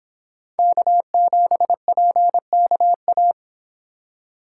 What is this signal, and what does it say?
Morse code, heard as a keyed tone, about 26 wpm, 700 Hz. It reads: K7PKA